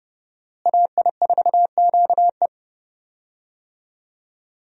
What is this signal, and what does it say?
Morse code, heard as a keyed tone, about 31 wpm, 700 Hz. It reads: AI4QE
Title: USA Call Signs - 5 Characters - Repeated 3x - 30wpm